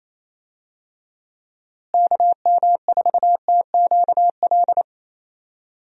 Morse code, heard as a keyed tone, about 28 wpm, 700 Hz. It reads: KM4TQL
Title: USA Call Signs - 6 Characters - Repeated 3x - 28wpm